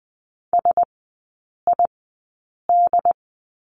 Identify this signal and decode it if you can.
Morse code, heard as a keyed tone, about 20 wpm, spaced as for 8 wpm, 700 Hz. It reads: SID
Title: 067 Lesson 12e - Letter d - Three Characters 20wpm